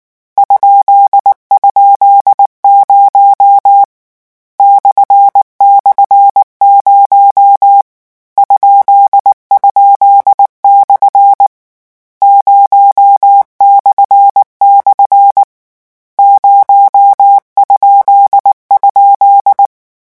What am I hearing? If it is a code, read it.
Morse code, heard as a keyed tone, about 19 wpm, 800 Hz. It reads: ??0 //0 ??/ 0// 0??
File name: morse mp3 / lecon13-vitesse 18